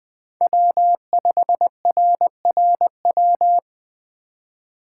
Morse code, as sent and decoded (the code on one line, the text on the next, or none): .-- ..... .-. .-. .--
W5RRW